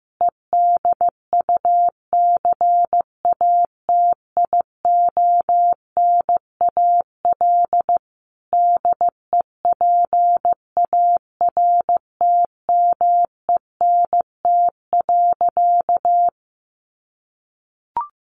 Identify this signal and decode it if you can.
Morse code, heard as a keyed tone, about 15 wpm, 700 Hz. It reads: EDUCATIONAL DEPARTMENT.
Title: Sentences from Top 1200 Words 15wpm